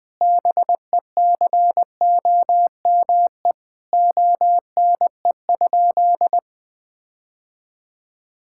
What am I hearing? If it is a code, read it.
Morse code, heard as a keyed tone, about 20 wpm, 700 Hz. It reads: BECOME ONE?
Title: Sentences from Top 1600 Words 20wpm